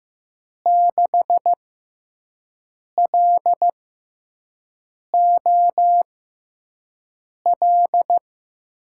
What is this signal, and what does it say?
Morse code, heard as a keyed tone, about 15 wpm, spaced as for 5 wpm, 700 Hz. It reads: 6LOL
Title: USA Call Signs - 5 Characters - 6x Character Spacing - 15wpm